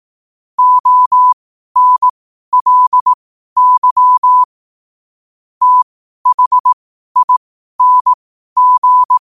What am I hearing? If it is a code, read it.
Morse code, heard as a keyed tone, about 18 wpm, spaced as for 12 wpm, 1000 Hz. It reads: ONLY THING